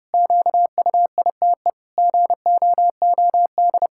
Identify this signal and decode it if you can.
Morse code, heard as a keyed tone, about 30 wpm, 700 Hz. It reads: QUITE GOOD
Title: Pangram Challenge v2 - 30wpm